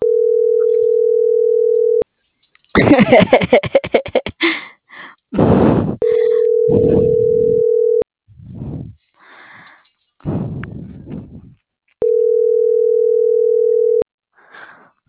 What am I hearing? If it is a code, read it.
Ringback tone (North America), 440 + 480 Hz.